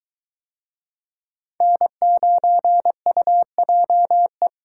Morse code, heard as a keyed tone, about 22 wpm, 700 Hz. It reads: N9UJE